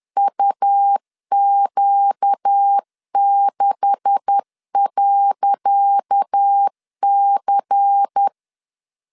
Morse code, heard as a keyed tone, about 10 wpm, 800 Hz. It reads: UQ6.C
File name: LEZIONI / 10 SIGNS / 10C DOT 10wpm 2940